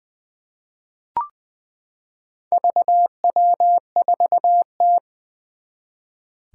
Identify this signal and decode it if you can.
Morse code, heard as a keyed tone, about 20 wpm, 700 Hz. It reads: VW4T
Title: Four Challenging Letter-Number Combinations 20wpm